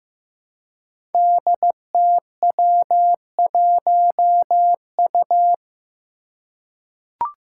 Morse code, heard as a keyed tone, about 15 wpm, 700 Hz. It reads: DTW1U
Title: Five Challenging Letter-Number Combinations 15wpm